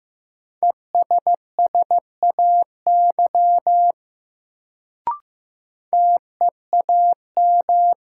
Morse code, heard as a keyed tone, about 15 wpm, 700 Hz. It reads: ESSAY TEAM
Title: Sentences from Top 1500 Words - New Word Drill Alt - 15wpm